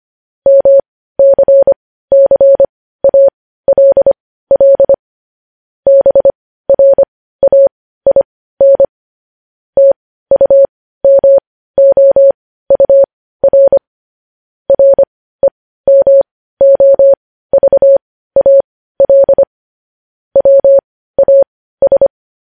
Morse code, heard as a keyed tone, about 25 wpm, spaced as for 15 wpm, 550 Hz. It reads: MCCALL BRAIN TUMOUR REMOVAL WAS
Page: News Headlines in Morse Code at 15 WPM